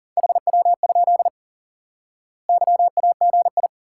Morse code, heard as a keyed tone, about 37 wpm, 700 Hz. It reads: HW? YAGI
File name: IQE Course 3 - Lesson 89 - YAGI - Two Elements - 40wpm